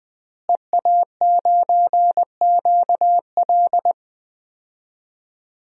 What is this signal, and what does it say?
Morse code, heard as a keyed tone, about 20 wpm, 700 Hz. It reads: EA9QL
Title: DXCC-100 Call Signs - 5 Characters - Repeated 3x - 20wpm